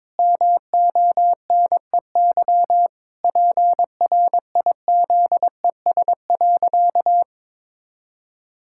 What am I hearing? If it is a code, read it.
Morse code, heard as a keyed tone, about 22 wpm, 700 Hz. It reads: MONEY PRIZES.
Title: Pangram Challenge v2 - 22wpm